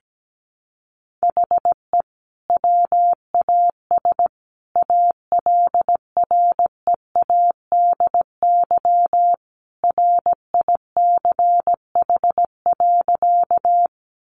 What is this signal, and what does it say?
Morse code, heard as a keyed tone, about 17 wpm, 700 Hz. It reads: HE WAS ALREADY RICH.
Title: Sentences from Top 1100 Words 17wpm